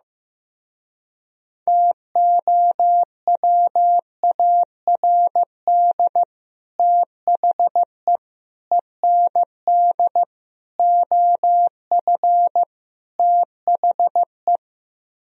Morse code, heard as a keyed tone, about 15 wpm, 700 Hz. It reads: TOWARD THE END OF THE
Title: Speed-Racing - Sets of 5 Words 15wpm